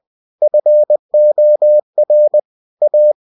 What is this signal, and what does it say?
Morse code, heard as a keyed tone, about 20 wpm, 600 Hz. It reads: FOR A